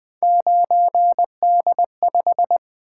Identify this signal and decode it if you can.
Morse code, heard as a keyed tone, about 20 wpm, 700 Hz. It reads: 9D5
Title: Three Letter-Number Combinations - 20wpm